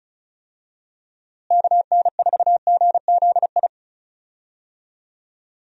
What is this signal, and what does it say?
Morse code, heard as a keyed tone, about 34 wpm, 700 Hz. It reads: KN4GZI
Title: USA Call Signs - 6 Characters - Repeated 3x - 35wpm